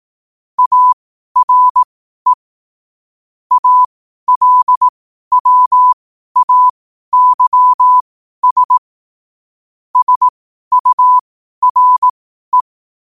Morse code, heard as a keyed tone, about 18 wpm, spaced as for 12 wpm, 1000 Hz. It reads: ARE ALWAYS SURE
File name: qotdfiles / 12wpm